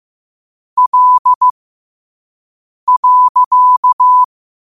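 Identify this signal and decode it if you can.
Morse code, heard as a keyed tone, about 15 wpm, spaced as for 5 wpm, 1000 Hz. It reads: L.
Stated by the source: Morse code quote of the day at 5 wpm.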